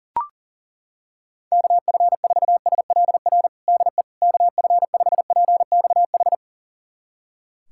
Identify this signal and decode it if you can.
Morse code, heard as a keyed tone, about 40 wpm, 700 Hz. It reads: KF4SLR DE KF5PXH
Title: USA Callsign DE Callsign - 6 Characters - 40wpm